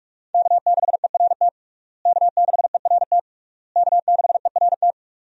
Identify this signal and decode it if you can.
Morse code, heard as a keyed tone, about 43 wpm, 700 Hz. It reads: K6ERT K6ERT K6ERT
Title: USA Call Signs - 5 Characters - Repeated 3x - 45wpm